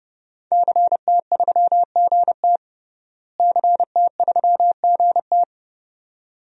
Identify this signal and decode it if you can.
Morse code, heard as a keyed tone, about 30 wpm, 700 Hz. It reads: CT3GT CT3GT
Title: DXCC-100 Call Signs - 5 Characters - Repeated 2x - 30wpm